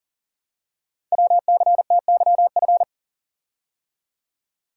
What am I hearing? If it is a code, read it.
Morse code, heard as a keyed tone, about 43 wpm, 700 Hz. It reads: WCTYF
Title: Code Groups - 5 Letter:Numbers - 40wpm